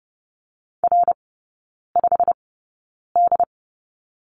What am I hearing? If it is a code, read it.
Morse code, heard as a keyed tone, about 30 wpm, spaced as for 9 wpm, 690 Hz. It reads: R5D
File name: Lesson 16e - Letter c - Three Characters 30wpm